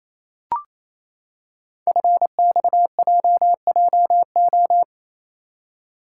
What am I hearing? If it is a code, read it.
Morse code, heard as a keyed tone, about 27 wpm, 700 Hz. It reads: FXJJO
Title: Code Groups - 5 Letters - 28wpm